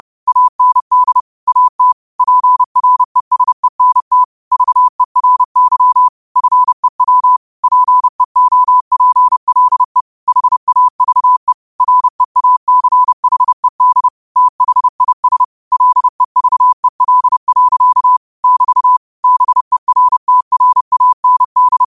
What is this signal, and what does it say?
Morse code, heard as a keyed tone, about 30 wpm, 1000 Hz. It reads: AND AT PRESENT VERY FEW PEOPLE HAVE REACHED THIS LEVEL. <BT> BERTRAND
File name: qotdfiles / 30wpm